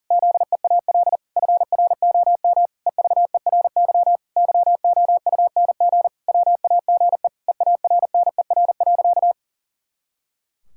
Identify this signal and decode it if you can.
Morse code, heard as a keyed tone, about 37 wpm, 700 Hz. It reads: ZEAL FROM EVERY YOUNG WAGE EARNER.
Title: Pangram Challenge v2 - 40wpm